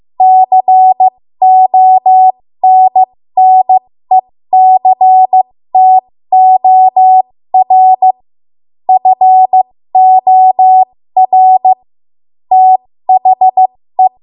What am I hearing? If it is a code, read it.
Morse code, heard as a keyed tone, about 15 wpm, spaced as for 13 wpm, 750 Hz. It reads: CONNECTOR FOR THE
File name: Morse / 13WPM